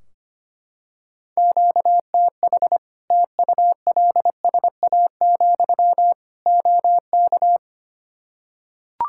Morse code, heard as a keyed tone, about 25 wpm, 700 Hz. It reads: QTH TULSA, OK
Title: Practice QSO - Exchange 1 - Beginning - 25wpm